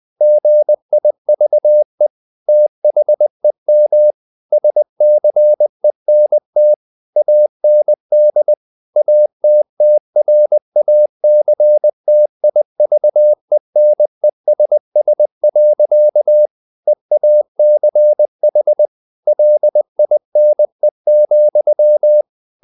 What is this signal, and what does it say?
Morse code, heard as a keyed tone, about 20 wpm, 600 Hz. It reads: GIVE THEM SCENT AND ATTRACTIVENESS. EACH LINE,
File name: the old man and the sea / the old man and the sea part 1 20wpm